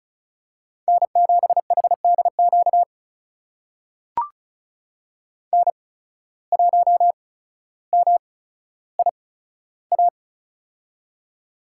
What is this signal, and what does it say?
Morse code, heard as a keyed tone, about 36 wpm, spaced as for 15 wpm, 700 Hz. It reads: N7HDQ N1MIA
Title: USA Call Signs - 5 Characters - 8x Character Spacing - 35wpm